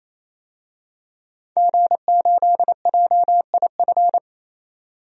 Morse code, heard as a keyed tone, about 28 wpm, 700 Hz. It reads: G8JIF
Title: DXCC-100 Call Signs - 5 Characters - 28wpm